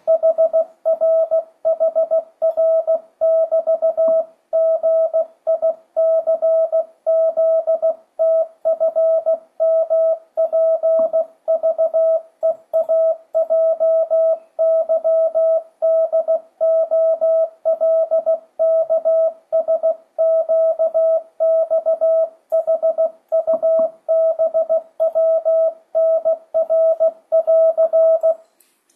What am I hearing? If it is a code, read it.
Morse code, heard as a keyed tone, about 16 wpm, 650 Hz. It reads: HRHR<BT>GICZTFMPVEAJYDOLKSQXHUBWNR<AR>
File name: 2025cw / speedup / 16wpm